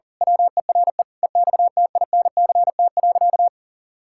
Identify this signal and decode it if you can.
Morse code, heard as a keyed tone, about 40 wpm, 700 Hz. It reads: WERE EXTINCT.
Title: Pangram Challenge v2 - 40wpm